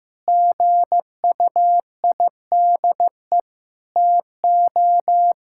Morse code, heard as a keyed tone, about 15 wpm, 700 Hz. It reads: GUIDE TO
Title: Sentences from Top 1000 Words - Review Part 6 - 15wpm